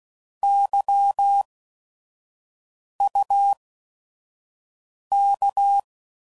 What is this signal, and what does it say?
Morse code, heard as a keyed tone, about 16 wpm, spaced as for 5 wpm, 800 Hz. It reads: YUK